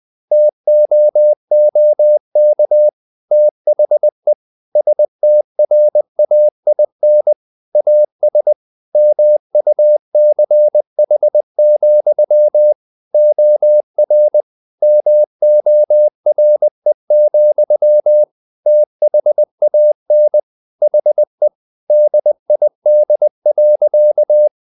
Morse code, heard as a keyed tone, about 20 wpm, 600 Hz. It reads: TOOK THE STRAIN AS MUCH, OR MORE, THAN HE DID.
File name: the old man and the sea part 2 20wpm